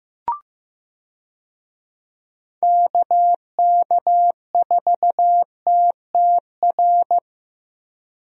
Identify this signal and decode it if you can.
Morse code, heard as a keyed tone, about 15 wpm, 700 Hz. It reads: KK4TTR
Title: USA Call Signs - 6 Characters - Repeated 2x - 15wpm